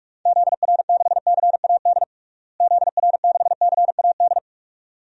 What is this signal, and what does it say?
Morse code, heard as a keyed tone, about 46 wpm, 700 Hz. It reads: ZR6CAD ZR6CAD
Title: DXCC-100 Call Signs - 6 Characters - Repeated 2x - 45wpm